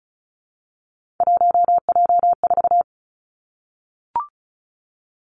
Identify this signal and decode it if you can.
Morse code, heard as a keyed tone, about 36 wpm, 700 Hz. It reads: 1J4
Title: Three Challenging Letter-Number Combinations 35wpm